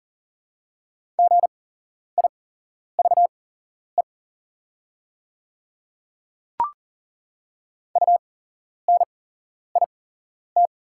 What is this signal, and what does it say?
Morse code, heard as a keyed tone, about 40 wpm, spaced as for 11 wpm, 690 Hz. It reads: GIVE UNIT
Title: 8x Character Spacing - Top 500 Words - 40wpm